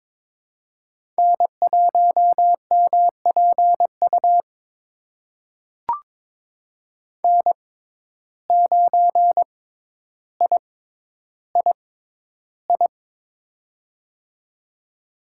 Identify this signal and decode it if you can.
Morse code, heard as a keyed tone, about 22 wpm, spaced as for 11 wpm, 700 Hz. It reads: N1MPU N9III
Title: USA Call Signs - 5 Characters - 6x Character Spacing - 22wpm